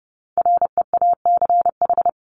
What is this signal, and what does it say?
Morse code, heard as a keyed tone, about 30 wpm, 700 Hz. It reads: REACH
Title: Top 500 Words Repeated 30wpm